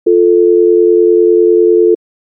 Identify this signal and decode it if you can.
Dial tone (North America), 350 + 440 Hz.